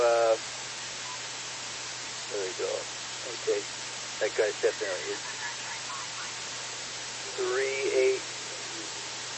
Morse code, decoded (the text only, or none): none